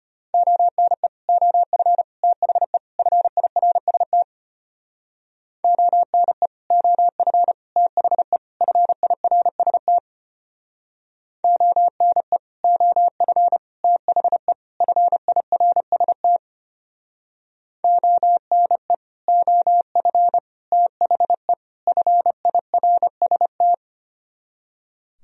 Morse code, heard as a keyed tone, about 33 wpm, 690 Hz. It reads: ONE OF THE FIRST ONE OF THE FIRST ONE OF THE FIRST ONE OF THE FIRST